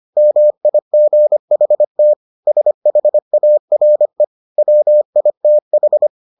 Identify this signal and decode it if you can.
Morse code, heard as a keyed tone, about 25 wpm, 600 Hz. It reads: MIGHT SHARE WITH